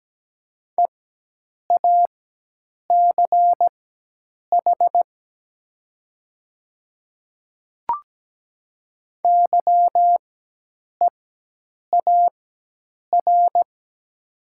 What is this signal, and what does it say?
Morse code, heard as a keyed tone, about 17 wpm, spaced as for 8 wpm, 700 Hz. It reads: EACH YEAR
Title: 4x Character Spacing - Top 200 Words - 17wpm